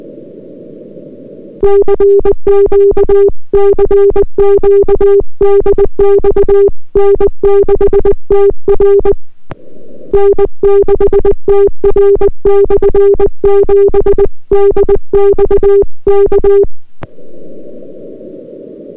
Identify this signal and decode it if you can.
Morse code, heard as a keyed tone, about 20 wpm, 400 Hz. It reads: CQ CQ DX N6TR N6TR/7 DX K